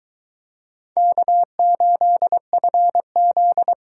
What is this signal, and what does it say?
Morse code, heard as a keyed tone, about 22 wpm, 700 Hz. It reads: K8FZ